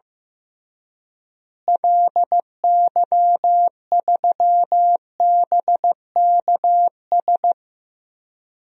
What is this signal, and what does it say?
Morse code, heard as a keyed tone, about 15 wpm, 700 Hz. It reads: LY3BKS
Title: DXCC-100 Call Signs - 6 Characters - Repeated 3x - 15wpm